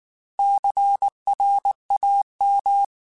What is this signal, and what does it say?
Morse code, heard as a keyed tone, about 19 wpm, 800 Hz. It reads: CRAM